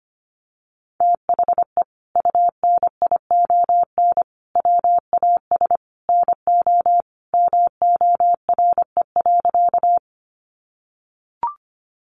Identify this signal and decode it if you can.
Morse code, heard as a keyed tone, about 25 wpm, 700 Hz. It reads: THE UNION WAS NO MORE.
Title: Sentences from Top 1400 Words 25wpm